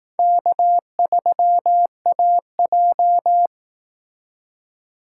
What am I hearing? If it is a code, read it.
Morse code, heard as a keyed tone, about 18 wpm, 700 Hz. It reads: K3AJ
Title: Speed Racing - USA 1x2 Call Signs 15wpm